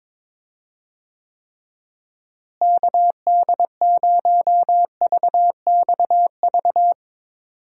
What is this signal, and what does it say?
Morse code, heard as a keyed tone, about 22 wpm, 700 Hz. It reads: KD0VXV
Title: USA Call Signs - 6 Characters - Repeated 3x - 22wpm